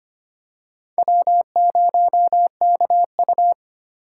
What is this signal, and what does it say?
Morse code, heard as a keyed tone, about 25 wpm, 700 Hz. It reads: W0KU
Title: USA Call Signs - 4 Characters - 3x Character Spacing - 25wpm